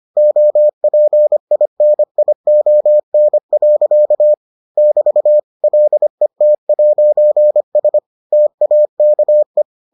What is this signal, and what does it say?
Morse code, heard as a keyed tone, about 25 wpm, 600 Hz. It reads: OPINION. <BT> LET'S TAKE